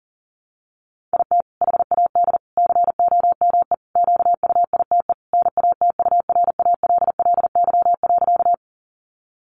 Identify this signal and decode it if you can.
Morse code, heard as a keyed tone, about 40 wpm, 700 Hz. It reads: IT HAD COME QUITE NATURALLY.
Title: Sentences from Top 1300 Words 40wpm